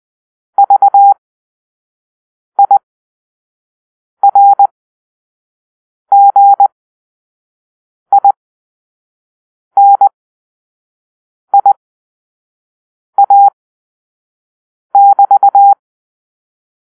Morse code, heard as a keyed tone, about 20 wpm, spaced as for 5 wpm, 800 Hz. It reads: VIRGINIA<BT>
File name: arrl / Grp-5 Drill-3